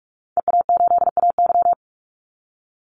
Morse code, heard as a keyed tone, about 44 wpm, 700 Hz. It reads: EA8AY
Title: Call Signs Repeated 3x - 45wpm